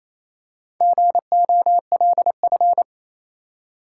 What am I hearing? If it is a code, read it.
Morse code, heard as a keyed tone, about 27 wpm, 690 Hz. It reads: GOLF
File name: Easy Words - Speed-Racing with Hints - 22wpm